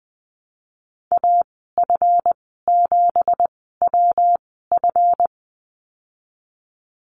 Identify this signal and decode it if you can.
Morse code, heard as a keyed tone, about 20 wpm, spaced as for 15 wpm, 700 Hz. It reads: AF7WF
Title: 173 Lesson 28g - Number 9 - Call Signs 20wpm